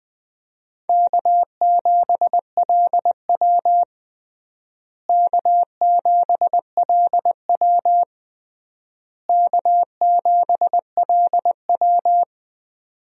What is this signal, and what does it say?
Morse code, heard as a keyed tone, about 20 wpm, 700 Hz. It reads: K7LW K7LW K7LW
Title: USA Call Signs - 4 Characters - Repeated 3x - 20wpm